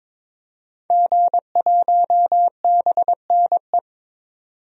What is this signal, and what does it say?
Morse code, heard as a keyed tone, about 22 wpm, 700 Hz. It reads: G1BNE
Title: DXCC-100 Call Signs - 5 Characters - 22wpm